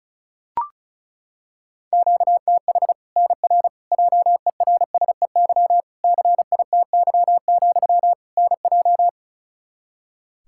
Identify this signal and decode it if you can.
Morse code, heard as a keyed tone, about 36 wpm, 700 Hz. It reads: QTH NR JERSEY CITY, NJ
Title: K1USN SST QSO Practice 35wpm